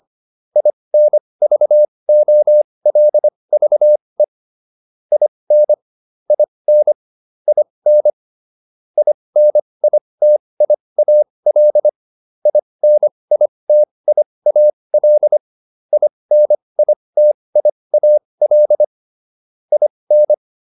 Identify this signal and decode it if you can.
Morse code, heard as a keyed tone, about 25 wpm, spaced as for 20 wpm, 600 Hz. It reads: INVOLVE IN IN IN INITIAL INITIAL INITIAL IN